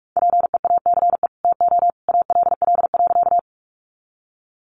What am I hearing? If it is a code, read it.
Morse code, heard as a keyed tone, about 44 wpm, 700 Hz. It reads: PEACE TO ALL.
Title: Sentences from Top 1500 Words 45wpm